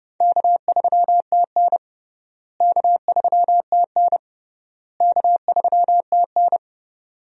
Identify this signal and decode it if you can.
Morse code, heard as a keyed tone, about 30 wpm, 700 Hz. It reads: K3TN K3TN K3TN
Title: USA Call Signs - 4 Characters - Repeated 3x - 30wpm